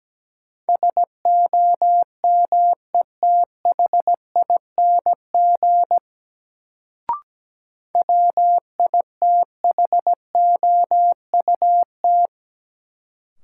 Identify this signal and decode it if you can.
Morse code, heard as a keyed tone, about 17 wpm, 700 Hz. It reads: SOMETHING WITHOUT
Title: Sentences from Top 600 Words - New Word Drill - 17wpm